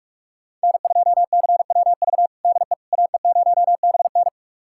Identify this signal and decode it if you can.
Morse code, heard as a keyed tone, about 42 wpm, 700 Hz. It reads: N2CWV DE AE0BN
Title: USA Callsign DE Callsign - 5 Characters - 45wpm